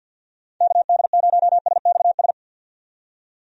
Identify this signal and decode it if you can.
Morse code, heard as a keyed tone, about 52 wpm, 700 Hz. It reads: KD0SXS
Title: USA Call Signs - 6 Characters - 50wpm